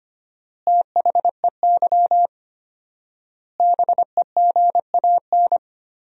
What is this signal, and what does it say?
Morse code, heard as a keyed tone, about 25 wpm, 700 Hz. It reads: THEY BEGAN
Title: 4x Word-Spacing - Sentences from Top 300 Words - 25wpm